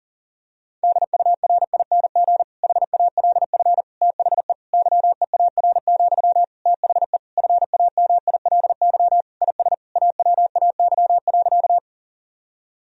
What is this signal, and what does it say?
Morse code, heard as a keyed tone, about 40 wpm, 700 Hz. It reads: DURING HALF THE YEAR, THE FAMILY IS AWAY.
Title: Sentences from Top 1000 Words - Review Part 2 - 40wpm